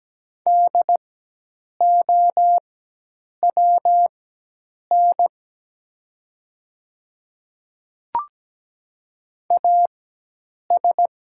Morse code, heard as a keyed tone, about 17 wpm, spaced as for 8 wpm, 700 Hz. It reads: DOWN AS
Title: 4x Character Spacing - Top 100 Words - 17wpm